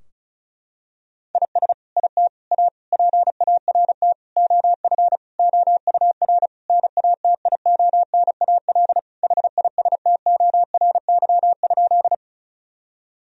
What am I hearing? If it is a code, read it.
Morse code, heard as a keyed tone, about 35 wpm, 700 Hz. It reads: IS IT A PART OF OUR NATIONAL HISTORY?
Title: Sentences from Top 1000 Words - Review Part 4 - 35wpm